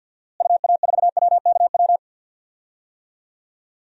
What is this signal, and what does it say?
Morse code, heard as a keyed tone, about 53 wpm, 700 Hz. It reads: UA4WKW
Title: DXCC-100 Call Signs - 6 Characters - Repeated 3x - 50wpm